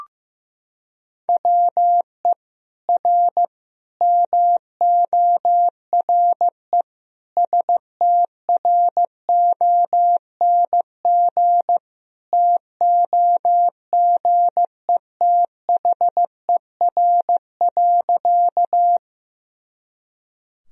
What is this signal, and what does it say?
Morse code, heard as a keyed tone, about 15 wpm, 700 Hz. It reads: WE R MORE STRONG TOGETHER.
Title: Sentences from Top 500 Words and Abbreviations 15wpm